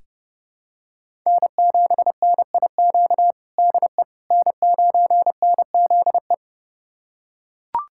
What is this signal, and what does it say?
Morse code, heard as a keyed tone, about 30 wpm, 700 Hz. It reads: N7NIQ DE N9NZE
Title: USA Callsign DE Callsign - 5 Characters - 30wpm